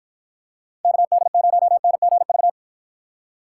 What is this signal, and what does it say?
Morse code, heard as a keyed tone, about 53 wpm, 700 Hz. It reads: KD0NGV